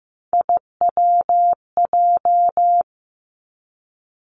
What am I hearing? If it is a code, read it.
Morse code, heard as a keyed tone, about 15 wpm, 700 Hz. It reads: IWJ
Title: Three Challenging Letter-Number Combinations 15wpm